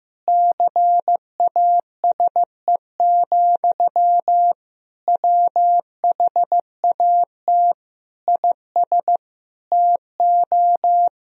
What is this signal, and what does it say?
Morse code, heard as a keyed tone, about 16 wpm, 700 Hz. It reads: CASE, WHAT IS TO
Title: Sentences from Top 1000 Words - Review Part 4 - 15wpm